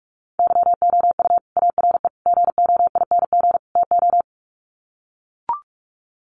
Morse code, heard as a keyed tone, about 44 wpm, 700 Hz. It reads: YOU ARE GOING TO
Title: Sets of 4 Words 45wpm